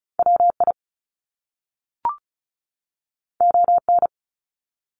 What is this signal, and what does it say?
Morse code, heard as a keyed tone, about 36 wpm, 700 Hz. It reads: WI ON
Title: Top Two Letter Combinations in Words 35wpm